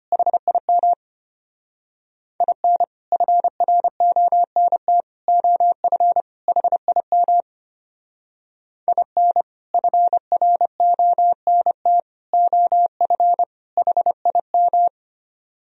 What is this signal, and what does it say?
Morse code, heard as a keyed tone, about 33 wpm, 700 Hz. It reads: HIM IN FRONT OF HIM IN FRONT OF HIM